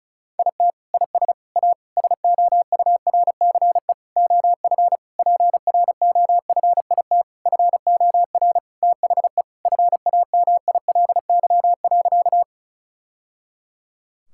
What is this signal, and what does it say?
Morse code, heard as a keyed tone, about 36 wpm, 700 Hz. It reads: IT IS A SOURCE OF PROFIT FOR THE FAMILY.
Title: Sentences from Top 1000 Words - Review Part 5 - 35wpm